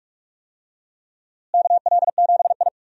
Morse code, heard as a keyed tone, about 46 wpm, 700 Hz. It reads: KL7I
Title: DXCC-100 Call Signs - 4 Characters - Repeated 3x - 45wpm